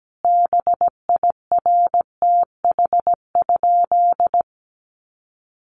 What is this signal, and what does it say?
Morse code, heard as a keyed tone, about 17 wpm, 700 Hz. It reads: BIRTH?
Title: Sentences from Top 800 Words 17wpm